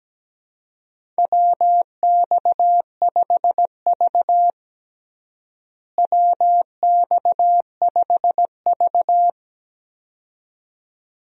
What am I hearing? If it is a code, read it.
Morse code, heard as a keyed tone, about 17 wpm, 700 Hz. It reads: WX5V WX5V
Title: USA Call Signs - 4 Characters - Repeated 2x - 17wpm